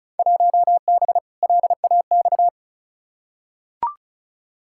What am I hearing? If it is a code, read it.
Morse code, heard as a keyed tone, about 36 wpm, 700 Hz. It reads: 1B LAX
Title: Field Day Exchanges - 35wpm